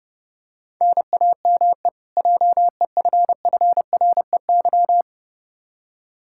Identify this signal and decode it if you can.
Morse code, heard as a keyed tone, about 31 wpm, 700 Hz. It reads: NAME JEFFREY
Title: Practice QSO - Exchange 1 - Beginning - 30wpm